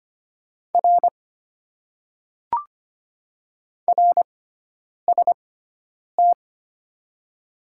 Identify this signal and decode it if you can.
Morse code, heard as a keyed tone, about 25 wpm, spaced as for 9 wpm, 690 Hz. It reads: R RST
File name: Top 90 QSO Elements - 6x Character Spacing - 25wpm